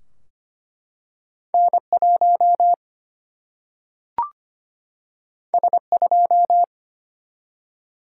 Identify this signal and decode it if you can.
Morse code, heard as a keyed tone, about 25 wpm, 700 Hz. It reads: N1 S2
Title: Single-Letter Single-Number 25wpm